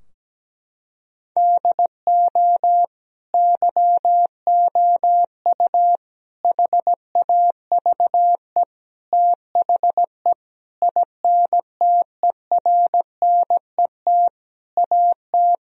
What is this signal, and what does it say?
Morse code, heard as a keyed tone, about 17 wpm, 700 Hz. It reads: DO YOU HAVE THE INTERNET AT
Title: Sentences from Top 1000 Words - Review Part 4 - 17wpm